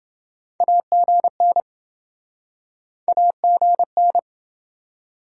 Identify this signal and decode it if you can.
Morse code, heard as a keyed tone, about 28 wpm, 700 Hz. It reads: AGN AGN